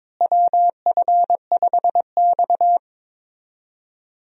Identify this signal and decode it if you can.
Morse code, heard as a keyed tone, about 22 wpm, 700 Hz. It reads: WF5X
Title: USA Call Signs - 4 Characters - Repeated 3x - 22wpm